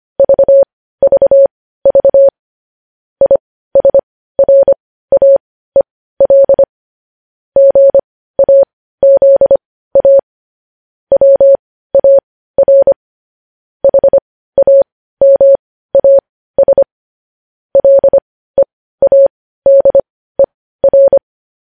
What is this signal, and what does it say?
Morse code, heard as a keyed tone, about 25 wpm, spaced as for 15 wpm, 550 Hz. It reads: VVV ISRAEL GAZA WAR HAMAS LEADER
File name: podcast / News headlines-1x-25x15